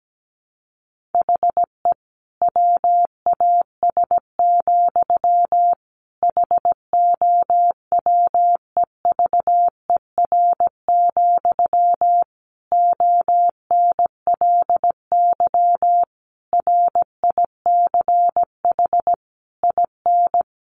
Morse code, heard as a keyed tone, about 17 wpm, 700 Hz. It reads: HE WAS, HOWEVER, ONLY RICH IN
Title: Sentences from Top 1100 Words 17wpm